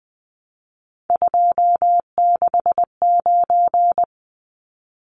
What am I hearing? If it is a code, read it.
Morse code, heard as a keyed tone, about 20 wpm, 700 Hz. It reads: 269
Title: Numbers - 3-Digit 20wpm